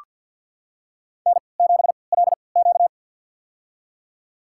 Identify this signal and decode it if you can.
Morse code, heard as a keyed tone, about 47 wpm, spaced as for 27 wpm, 700 Hz. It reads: N7LQ
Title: USA Call Signs - 4 Characters - 3x Character Spacing - 50wpm